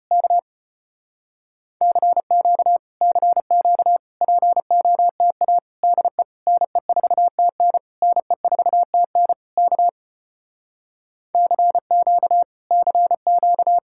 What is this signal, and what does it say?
Morse code, heard as a keyed tone, about 33 wpm, 700 Hz. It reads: K CQ CQ POTA DE NE4TN NE4TN K CQ CQ